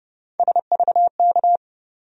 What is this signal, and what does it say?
Morse code, heard as a keyed tone, about 30 wpm, 700 Hz. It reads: SVK
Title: USA Call Sign Postfixes - 3 Letters 30wpm